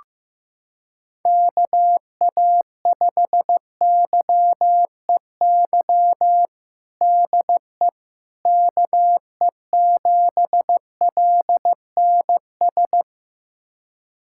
Morse code, heard as a keyed tone, about 16 wpm, 700 Hz. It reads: KA5YEY DE KE7LNS
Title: USA Callsign DE Callsign 15wpm